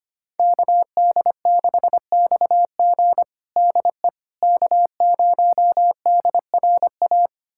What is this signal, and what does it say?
Morse code, heard as a keyed tone, about 25 wpm, 700 Hz. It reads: KD6XG DE K0DRA
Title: USA Callsign DE Callsign - 5 Characters - 25wpm